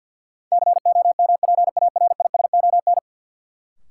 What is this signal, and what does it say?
Morse code, heard as a keyed tone, about 48 wpm, 700 Hz. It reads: COMPARISON
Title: Sentences from Top 1600 Words - New Word Drill Alt - 50wpm